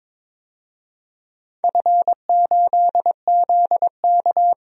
Morse code, heard as a keyed tone, about 21 wpm, 700 Hz. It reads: F8ZK
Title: DXCC-100 Call Signs - 4 Characters - 22wpm